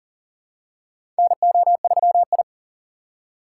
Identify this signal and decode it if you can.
Morse code, heard as a keyed tone, about 40 wpm, 700 Hz. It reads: NO3I
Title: USA Call Signs - 2x Character Spacing - 40wpm